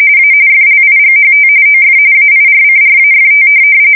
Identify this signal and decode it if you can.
RTTY, about 45.45 baud. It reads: A FLIGHT OF AIRCRAFT
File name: fsk170 / USS-New-Jersey